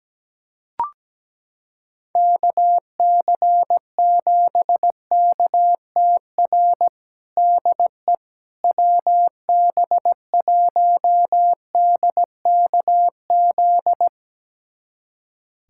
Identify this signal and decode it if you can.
Morse code, heard as a keyed tone, about 17 wpm, 700 Hz. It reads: KC7KTR DE WB1DKZ
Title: USA Callsign DE Callsign 17wpm